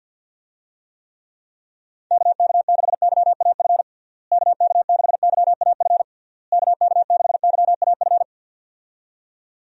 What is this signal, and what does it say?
Morse code, heard as a keyed tone, about 51 wpm, 700 Hz. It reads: KK6YAF KK6YAF KK6YAF
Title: USA Call Signs - 6 Characters - Repeated 3x - 50wpm